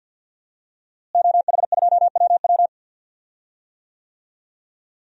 Morse code, heard as a keyed tone, about 47 wpm, 700 Hz. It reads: OH2WW
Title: DXCC-100 Call Signs - 5 Characters - Repeated 3x - 50wpm